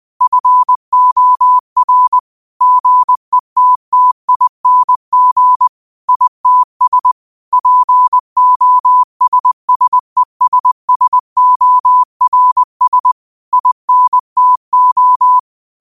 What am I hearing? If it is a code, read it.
Morse code, heard as a keyed tone, about 20 wpm, 1000 Hz. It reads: FOR GETTING ITS POSSESSORS INTO
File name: qotdfiles / 20wpm